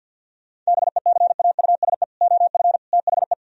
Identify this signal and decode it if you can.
Morse code, heard as a keyed tone, about 60 wpm, 700 Hz. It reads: BECAUSE OF THE
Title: Speed-Racing - Sets of 3 Words 50wpm